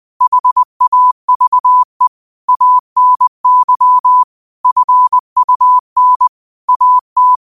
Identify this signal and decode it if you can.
Morse code, heard as a keyed tone, about 20 wpm, 1000 Hz. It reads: HAVE ANY FUN AT